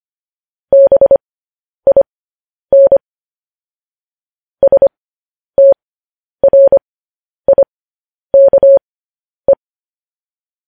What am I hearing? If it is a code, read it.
Morse code, heard as a keyed tone, about 25 wpm, spaced as for 10 wpm, 550 Hz. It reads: BIN STRIKE
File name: podcast / News headlines-1x-25x10